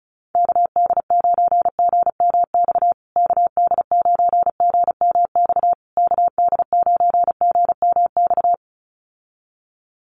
Morse code, heard as a keyed tone, about 34 wpm, 700 Hz. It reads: KD9GMX KD9GMX KD9GMX
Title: USA Call Signs Repeated 3x 35wpm